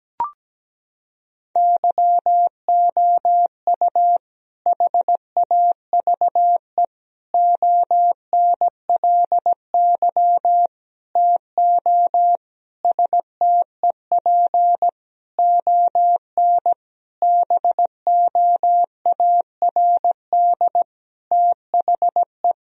Morse code, heard as a keyed tone, about 17 wpm, 700 Hz. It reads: YOU HAVE ONLY TO STEP ON BOARD THE
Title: Sentences from Top 1000 Words - Review Part 4 - 17wpm